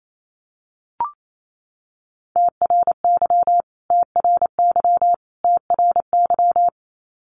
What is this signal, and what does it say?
Morse code, heard as a keyed tone, about 28 wpm, 700 Hz. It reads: TRY TRY TRY
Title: Top 500 Words Repeated 28wpm